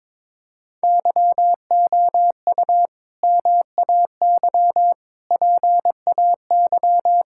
Morse code, heard as a keyed tone, about 22 wpm, 700 Hz. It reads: YOU MAY PAY
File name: practice / 22wpm / Sets of 8 Words - 22wpm